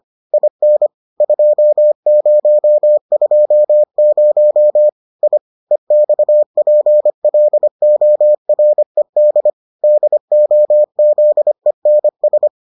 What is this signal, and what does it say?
Morse code, heard as a keyed tone, about 25 wpm, 600 Hz. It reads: IN 2020 I EXPLORED DOZENS